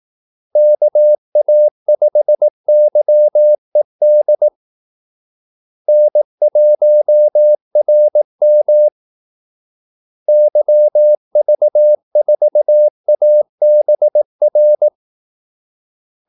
Morse code, heard as a keyed tone, about 18 wpm, 600 Hz. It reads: KA5YED N1RM YV4ABR
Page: Kaldesignaler 18 wpm